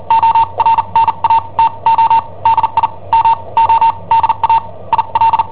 Morse code, heard as a keyed tone, about 43 wpm, spaced as for 31 wpm, 910 Hz. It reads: ORNATO DI MODA IL
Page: Cassetta esercizi preparata da I7OHP per IW5ELP